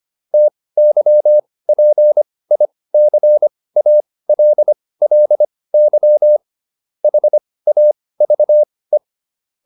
Morse code, heard as a keyed tone, about 25 wpm, spaced as for 18 wpm, 600 Hz. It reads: TYPICALLY HAVE